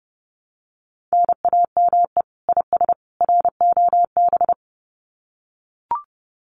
Morse code, heard as a keyed tone, about 30 wpm, 700 Hz. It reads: NAME IS ROB
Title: Common QSO Phrases 30wpm